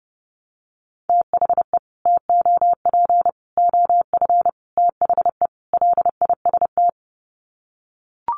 Morse code, heard as a keyed tone, about 28 wpm, 700 Hz. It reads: THE TOP OF THE LIST
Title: Sets of 5 Words Encore2 30wpm